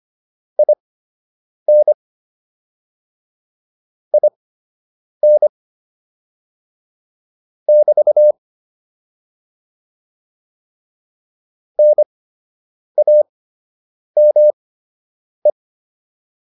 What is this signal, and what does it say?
Morse code, heard as a keyed tone, about 25 wpm, spaced as for 8 wpm, 600 Hz. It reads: IN IN <BT> NAME